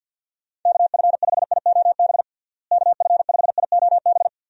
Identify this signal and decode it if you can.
Morse code, heard as a keyed tone, about 50 wpm, 700 Hz. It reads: KF5IOB KF5IOB
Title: USA Call Signs - 6 Characters - Repeated 2x - 50wpm